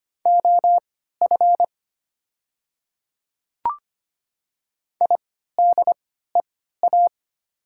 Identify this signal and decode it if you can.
Morse code, heard as a keyed tone, about 25 wpm, spaced as for 14 wpm, 700 Hz. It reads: OF IDEA
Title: 3x Character Spacing - Top 300 Words - 25wpm